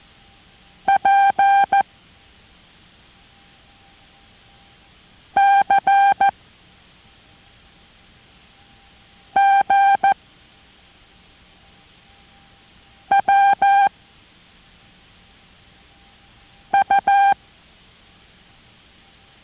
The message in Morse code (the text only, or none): P C G W U